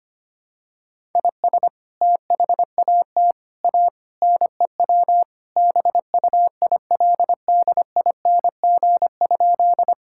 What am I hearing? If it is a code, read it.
Morse code, heard as a keyed tone, about 25 wpm, 700 Hz. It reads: IS THAT A NEW BUILDING?
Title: Sentences from Top 1000 Words - Review Part 5 - 25wpm